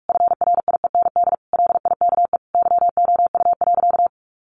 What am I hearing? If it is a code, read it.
Morse code, heard as a keyed tone, about 44 wpm, 700 Hz. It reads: FRIEND LIKE YOU.
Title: Sentences from Top 1100 Words 45wpm